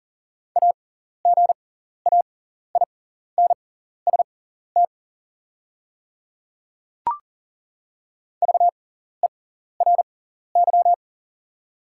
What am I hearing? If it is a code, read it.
Morse code, heard as a keyed tone, about 40 wpm, spaced as for 14 wpm, 700 Hz. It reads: AGAINST VERY
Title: 6x Character Spacing - Top 500 Words - 40wpm